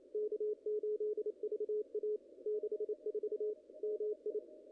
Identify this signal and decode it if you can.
Morse code, heard as a keyed tone, about 28 wpm, 440 Hz. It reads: K8VA 64 MI
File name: contests / ss-cw-k5tr-op-n5rz -0000